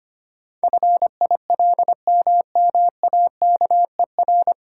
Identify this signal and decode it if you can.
Morse code, heard as a keyed tone, about 25 wpm, 700 Hz. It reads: FILMMAKER
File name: practice / 25wpm / 9 Letter Words - First 3 Letters Given - 25wpm